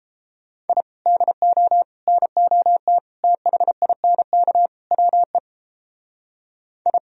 Morse code, heard as a keyed tone, about 30 wpm, 700 Hz. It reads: I DO NOT THINK WE I